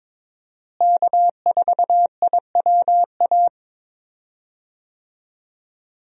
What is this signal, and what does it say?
Morse code, heard as a keyed tone, about 22 wpm, 700 Hz. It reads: K4IWA
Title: USA Call Signs - 5 Characters - Repeated 3x - 22wpm